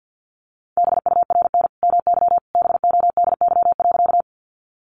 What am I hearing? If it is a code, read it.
Morse code, heard as a keyed tone, about 50 wpm, 700 Hz. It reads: BURN MY BODY.
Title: Sentences from Top 1300 Words 50wpm